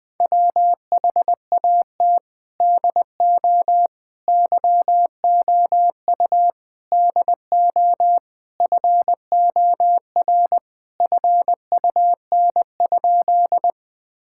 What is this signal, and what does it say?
Morse code, heard as a keyed tone, about 21 wpm, 700 Hz. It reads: WHAT DO YOU DO FOR FUN?
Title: Sentences with 4-Word Phrases - 20wpm